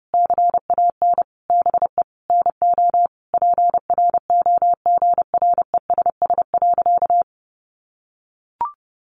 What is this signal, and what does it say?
Morse code, heard as a keyed tone, about 28 wpm, 700 Hz. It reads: CAN BE NO PROGRESS.
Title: Sentences from Top 1400 Words 30wpm